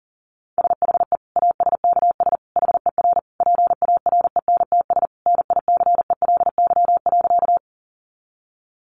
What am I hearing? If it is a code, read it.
Morse code, heard as a keyed tone, about 40 wpm, 700 Hz. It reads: SHE ASKS HER PARENTS NICELY.
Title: Easy Phrases 40wpm